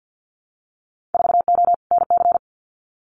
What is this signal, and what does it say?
Morse code, heard as a keyed tone, about 50 wpm, 700 Hz. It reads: 4O NC
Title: Winter Field Day Exchanges 50wpm